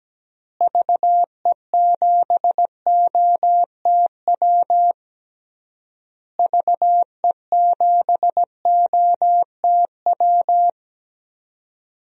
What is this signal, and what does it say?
Morse code, heard as a keyed tone, about 17 wpm, 700 Hz. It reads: VE7OTW VE7OTW
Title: DXCC-100 Call Signs - 6 Characters - Repeated 3x - 17wpm